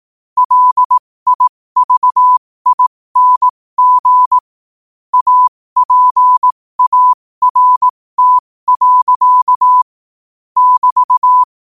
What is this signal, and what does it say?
Morse code, heard as a keyed tone, about 18 wpm, spaced as for 15 wpm, 1000 Hz. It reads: LIVING APART. <BT>
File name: qotdfiles / 15wpm